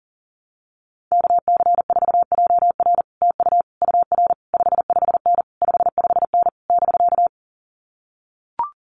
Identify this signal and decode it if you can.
Morse code, heard as a keyed tone, about 40 wpm, 700 Hz. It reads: KC4JR TU UR 55N 55N <BK>
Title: Common QSO Phrases Encore 40wpm